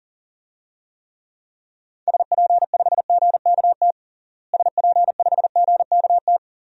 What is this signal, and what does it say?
Morse code, heard as a keyed tone, about 37 wpm, 700 Hz. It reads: SP5GKT SP5GKT
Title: DXCC-100 Call Signs - 6 Characters - Repeated 3x - 40wpm